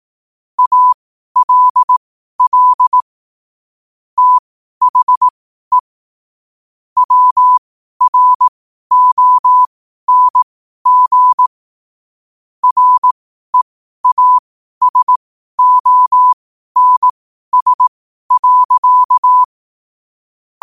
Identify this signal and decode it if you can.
Morse code, heard as a keyed tone, about 18 wpm, spaced as for 12 wpm, 1000 Hz. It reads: ALL THE WRONG REASONS.
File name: 12wpm